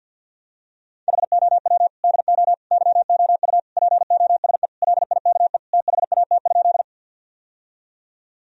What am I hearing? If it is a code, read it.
Morse code, heard as a keyed tone, about 50 wpm, 700 Hz. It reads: HOW DO YOU POSE LIKE THAT?
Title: Sentences from Top 1000 Words - Review Part 2 - 50wpm